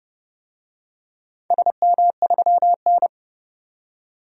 Morse code, heard as a keyed tone, about 30 wpm, 700 Hz. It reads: SM3N
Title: DXCC-100 Call Signs - 4 Characters - 30wpm